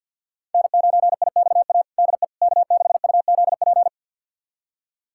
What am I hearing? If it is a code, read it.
Morse code, heard as a keyed tone, about 46 wpm, 700 Hz. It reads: N9IXA DE K6UZP
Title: USA Callsign DE Callsign - 5 Characters - 50wpm